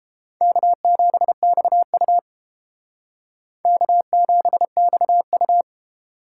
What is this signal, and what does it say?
Morse code, heard as a keyed tone, about 30 wpm, 700 Hz. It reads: K7XU K7XU